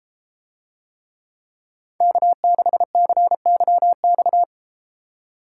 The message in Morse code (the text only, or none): K6CYX